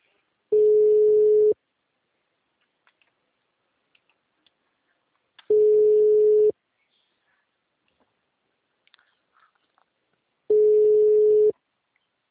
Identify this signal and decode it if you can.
Ringback tone (Europe, 425 Hz).